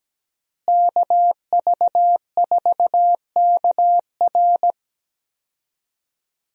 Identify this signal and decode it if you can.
Morse code, heard as a keyed tone, about 17 wpm, 700 Hz. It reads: KV4KR
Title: USA Call Signs - 5 Characters - Repeated 3x - 17wpm